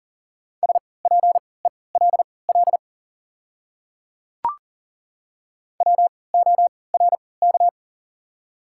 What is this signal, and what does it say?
Morse code, heard as a keyed tone, about 40 wpm, spaced as for 23 wpm, 700 Hz. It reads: SPELL WORK
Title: 3x Character Spacing - Top 200 Words - 40wpm